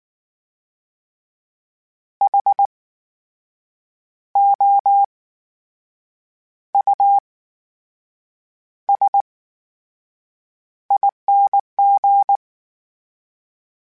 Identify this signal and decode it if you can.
Morse code, heard as a keyed tone, about 19 wpm, spaced as for 6 wpm, 800 Hz. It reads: HOUSING
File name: Special INGs / Special INGs-mp3s / VerySlowToFaster Ing @18